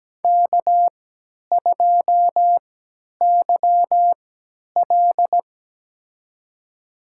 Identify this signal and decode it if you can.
Morse code, heard as a keyed tone, about 17 wpm, spaced as for 10 wpm, 700 Hz. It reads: K2YL
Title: USA Call Signs - 3x Character Spacing - 17wpm